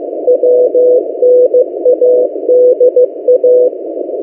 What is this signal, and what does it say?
Morse code, heard as a keyed tone, about 15 wpm, 500 Hz. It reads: WNADA
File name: M08a- -1355utc-8096khz